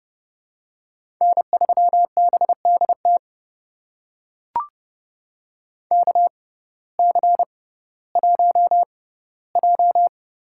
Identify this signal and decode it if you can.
Morse code, heard as a keyed tone, about 31 wpm, spaced as for 17 wpm, 700 Hz. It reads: N3BDT KC1J
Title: USA Call Signs - 6x Character Spacing - 30wpm